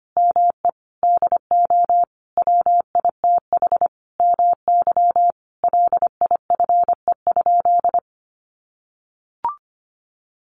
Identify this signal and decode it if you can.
Morse code, heard as a keyed tone, about 25 wpm, 700 Hz. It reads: ME DO WITH MY LIFE?
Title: Sentences from Top 300 Words 25wpm